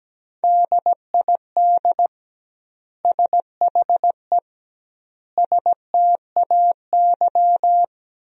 Morse code, heard as a keyed tone, about 17 wpm, 700 Hz. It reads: DID SHE STAY
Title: 2x Word-Spacing - Sentences from Top 500 Words - 17wpm